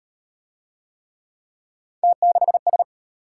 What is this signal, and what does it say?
Morse code, heard as a keyed tone, about 32 wpm, 700 Hz. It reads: T6S